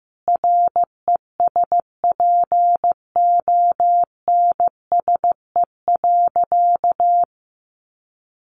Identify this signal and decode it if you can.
Morse code, heard as a keyed tone, about 15 wpm, 700 Hz. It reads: RESPONSE.
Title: Sentences from Top 1400 Words 15wpm